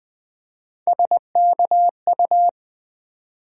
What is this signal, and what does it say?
Morse code, heard as a keyed tone, about 20 wpm, 700 Hz. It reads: SKU
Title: Build A Word - 1 Letter At A Time - 3 to 5-Letter Words - 20wpm